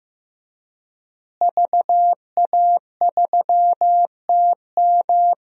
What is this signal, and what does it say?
Morse code, heard as a keyed tone, about 15 wpm, 700 Hz. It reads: VA3TM
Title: DXCC-100 Call Signs - 5 Characters - Repeated 2x - 15wpm